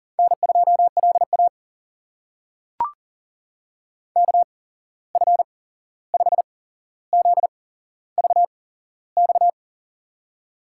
Morse code, heard as a keyed tone, about 40 wpm, spaced as for 15 wpm, 700 Hz. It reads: N2LA KF5ZVX
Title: USA Call Signs - 8x Character Spacing - 40wpm